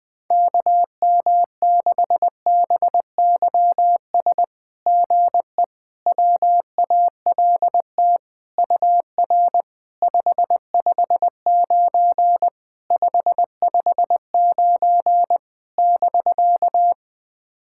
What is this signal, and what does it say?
Morse code, heard as a keyed tone, about 20 wpm, 690 Hz. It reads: KM6BYS GE WALT UR 559 559 <BK>
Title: SOTA QSO Practice 20wpm